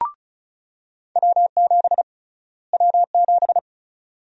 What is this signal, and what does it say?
Morse code, heard as a keyed tone, about 36 wpm, 700 Hz. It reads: W7 W7
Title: USA Call Sign Prefixes - Repeated 2x - 35wpm